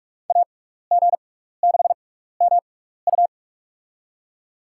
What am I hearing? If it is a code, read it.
Morse code, heard as a keyed tone, about 43 wpm, spaced as for 15 wpm, 700 Hz. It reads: AG6MU
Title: USA Call Signs - 6x Character Spacing - 45wpm